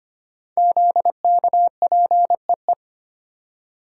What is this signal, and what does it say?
Morse code, heard as a keyed tone, about 25 wpm, 700 Hz. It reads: ZKPEE
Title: Code Groups - 5 Letters - 25wpm